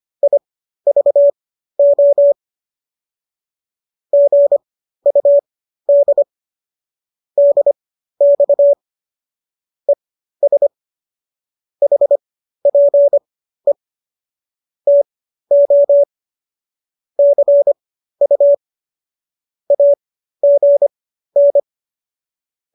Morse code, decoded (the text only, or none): IVO GUD DX ES HPE TO CU AGN